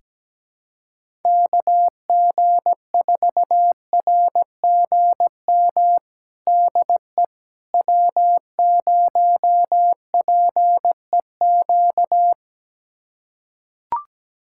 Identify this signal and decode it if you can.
Morse code, heard as a keyed tone, about 17 wpm, 700 Hz. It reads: KG4RGM DE W0PEQ
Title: USA Callsign DE Callsign 17wpm